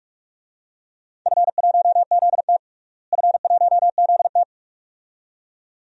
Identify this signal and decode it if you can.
Morse code, heard as a keyed tone, about 46 wpm, 700 Hz. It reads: F1ZT F1ZT
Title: DXCC-100 Call Signs - 4 Characters - Repeated 2x - 45wpm